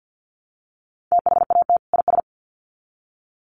Morse code, heard as a keyed tone, about 50 wpm, 700 Hz. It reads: THAT IS
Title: Sets of 2 Words 50wpm